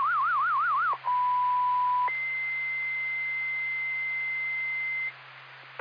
Two-tone page, heard as a one-second tone, then a three-second tone, about 990 then 2050 Hz, with noise. It is heard from 1.1 to 5.1 s.
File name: Pochuck FD 2026 07 26 14 55 56